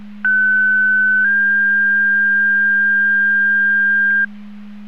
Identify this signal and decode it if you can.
Two-tone page, heard as a one-second tone, then a three-second tone, about 1500 then 1600 Hz. It is heard from 0.2 to 4.2 s.